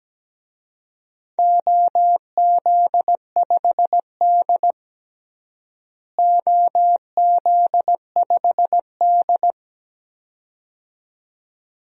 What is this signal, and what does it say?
Morse code, heard as a keyed tone, about 17 wpm, 700 Hz. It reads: OZ5D OZ5D
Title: DXCC-100 Call Signs - 4 Characters - Repeated 2x - 17wpm